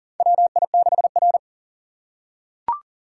Morse code, heard as a keyed tone, about 39 wpm, 700 Hz. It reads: WI6R
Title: USA Call Signs - 4 Characters - 2x Character Spacing - 40wpm